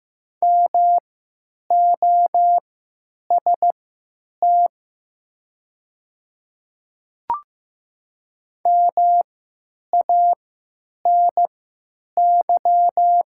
Morse code, heard as a keyed tone, about 15 wpm, spaced as for 9 wpm, 700 Hz. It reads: MOST MANY
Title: 3x Character Spacing - Top 100 Words - 15wpm